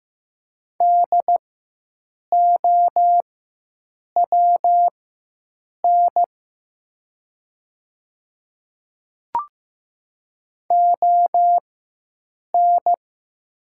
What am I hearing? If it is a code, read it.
Morse code, heard as a keyed tone, about 15 wpm, spaced as for 7 wpm, 700 Hz. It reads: DOWN ON
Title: 4x Character Spacing - Top 100 Words - 15wpm